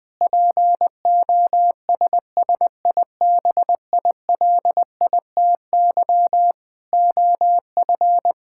Morse code, heard as a keyed tone, about 20 wpm, 700 Hz. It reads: POSSIBILITY OF
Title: Sentences with 8-Word Phrases - 20wpm